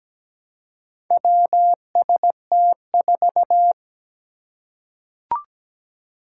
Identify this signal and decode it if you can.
Morse code, heard as a keyed tone, about 17 wpm, 700 Hz. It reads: WST4
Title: Four Challenging Letter-Number Combinations 17wpm